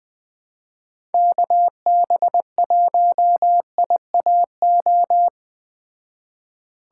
Morse code, heard as a keyed tone, about 19 wpm, 700 Hz. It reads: KB1IAO